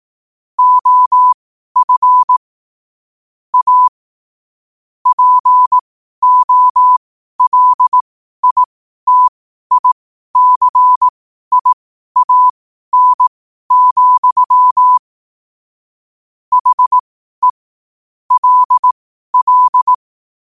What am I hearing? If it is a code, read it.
Morse code, heard as a keyed tone, about 18 wpm, spaced as for 12 wpm, 1000 Hz. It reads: OF A POLITICIAN, HE LL